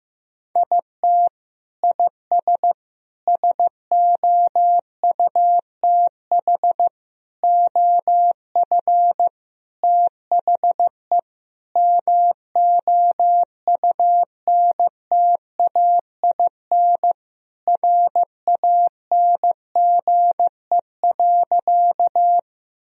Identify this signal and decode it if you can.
Morse code, heard as a keyed tone, about 15 wpm, 700 Hz. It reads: IT IS SOUTH OF THE MOUNTAIN RANGE.
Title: Sentences from Top 1000 Words - Review Part 4 - 15wpm